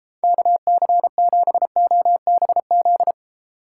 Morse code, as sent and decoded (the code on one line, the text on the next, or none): -.- -.-. --... --- -... --..
KC7OBZ